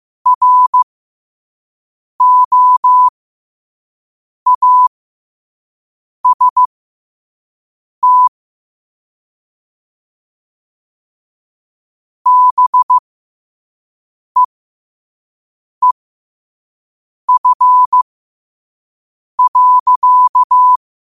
Morse code, heard as a keyed tone, about 15 wpm, spaced as for 5 wpm, 1000 Hz. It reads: ROAST BEEF.
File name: qotdfiles / 5wpm